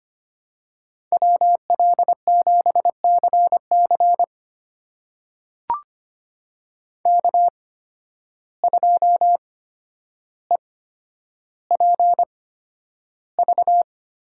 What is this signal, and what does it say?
Morse code, heard as a keyed tone, about 25 wpm, spaced as for 11 wpm, 700 Hz. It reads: WL7CC K2EPV
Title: USA Call Signs - 8x Character Spacing - 25wpm